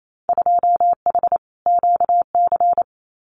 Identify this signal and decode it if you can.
Morse code, heard as a keyed tone, about 28 wpm, 700 Hz. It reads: 2H QC
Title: Winter Field Day Exchanges 28wpm